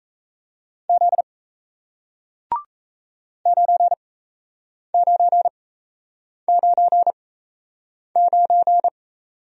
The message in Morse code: Z 9999

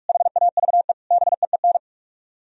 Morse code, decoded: HAVE BEEN

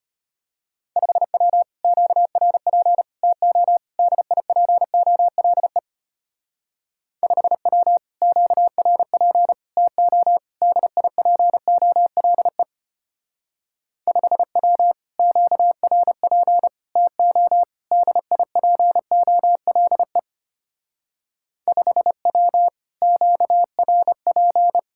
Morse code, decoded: 5W QRP TO DIPOLE 5W QRP TO DIPOLE 5W QRP TO DIPOLE 5W QRP

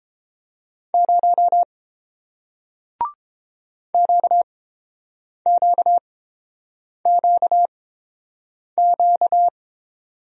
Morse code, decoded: 0 QQQQ

